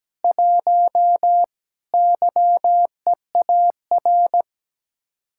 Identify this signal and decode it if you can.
Morse code, heard as a keyed tone, about 17 wpm, 700 Hz. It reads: 1 YEAR
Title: Speed-Racing Common QSO Phrases - 17wpm